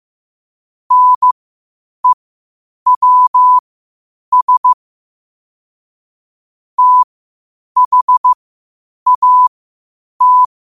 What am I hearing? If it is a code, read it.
Morse code, heard as a keyed tone, about 15 wpm, spaced as for 8 wpm, 1000 Hz. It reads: NEWS THAT
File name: qotdfiles / 8wpm